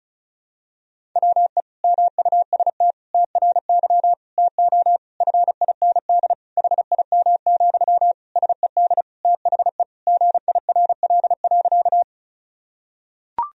Morse code, heard as a keyed tone, about 36 wpm, 700 Hz. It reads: WE MUST TRY TO FIND HIM, SED THE GIRL.
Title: Sentences from Top 400 Words and Abbreviations 35wpm